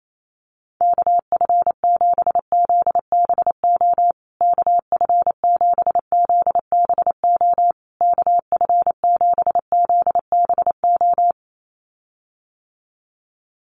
Morse code, heard as a keyed tone, about 28 wpm, 700 Hz. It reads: KF7ZBO KF7ZBO KF7ZBO
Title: USA Call Signs Repeated 3x 28wpm